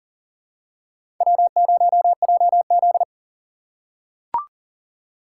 Morse code, heard as a keyed tone, about 37 wpm, 700 Hz. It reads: W0JZ
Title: USA Call Signs - 4 Characters - 4x Character Spacing - 40wpm